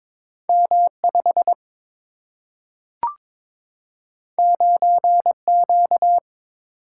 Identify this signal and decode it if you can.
Morse code, heard as a keyed tone, about 22 wpm, 700 Hz. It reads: M5 9Q
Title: Single-Letter Single-Number 22wpm